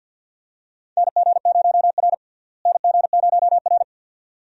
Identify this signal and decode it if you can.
Morse code, heard as a keyed tone, about 52 wpm, 700 Hz. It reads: NG0R NG0R
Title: USA Call Signs - 4 Characters - Repeated 2x - 50wpm